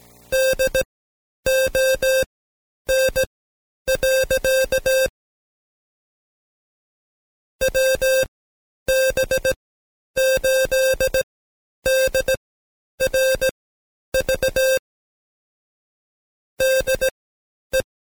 Morse code, heard as a keyed tone, about 18 wpm, spaced as for 10 wpm, 530 Hz. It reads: DON. WB8DRV DE
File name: Code Quick / AudioDownld2 / Seg131-Stepper4 9WPM